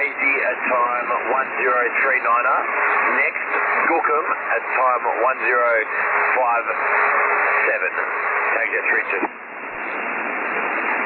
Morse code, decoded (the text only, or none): none